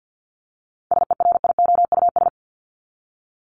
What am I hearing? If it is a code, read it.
Morse code, heard as a keyed tone, about 48 wpm, 700 Hz. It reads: SERIOUS
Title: LY Suffix - Top 100 Words with Base Words 50wpm